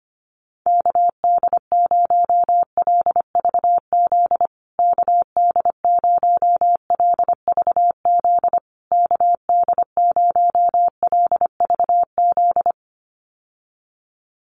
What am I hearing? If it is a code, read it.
Morse code, heard as a keyed tone, about 25 wpm, 700 Hz. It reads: KD0LVZ KD0LVZ KD0LVZ
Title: USA Call Signs Repeated 3x 25wpm